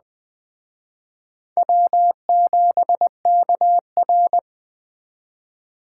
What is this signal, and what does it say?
Morse code, heard as a keyed tone, about 20 wpm, 700 Hz. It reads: W7KR
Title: Speed Racing - USA 1x2 Call Signs 17wpm